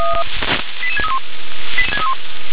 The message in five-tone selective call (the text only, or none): 88631